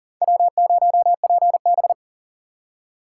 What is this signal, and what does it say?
Morse code, heard as a keyed tone, about 40 wpm, 690 Hz. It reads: W0PB
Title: USA Call Signs - 3x Character Spacing - 40wpm